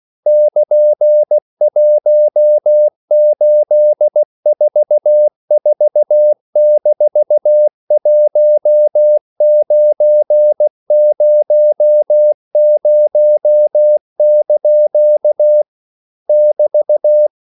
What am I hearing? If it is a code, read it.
Morse code, heard as a keyed tone, about 16 wpm, 600 Hz. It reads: <KN>1844-1900) <BT>
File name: CW / Practice / EnglishQuotations0116